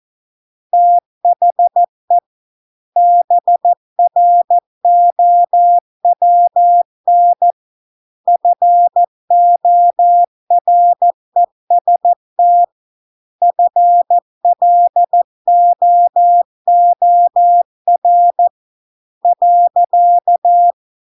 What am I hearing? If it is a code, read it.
Morse code, heard as a keyed tone, about 14 wpm, 700 Hz. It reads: THE BROWN FOREST FLOOR .